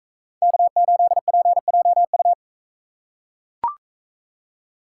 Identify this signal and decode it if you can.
Morse code, heard as a keyed tone, about 43 wpm, 700 Hz. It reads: K8PJU